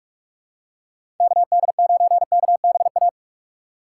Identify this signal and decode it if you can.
Morse code, heard as a keyed tone, about 45 wpm, 700 Hz. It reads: KD9KBA
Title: USA Call Signs - 4x Character Spacing - 45wpm